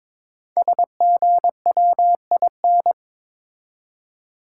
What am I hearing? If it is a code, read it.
Morse code, heard as a keyed tone, about 22 wpm, 700 Hz. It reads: SGWIN
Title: Code Groups - 5 Letters - 22wpm